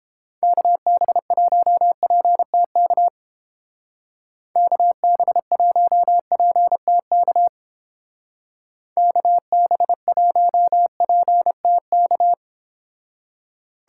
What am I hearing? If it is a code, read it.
Morse code, heard as a keyed tone, about 32 wpm, 700 Hz. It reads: KB1PTK KB1PTK KB1PTK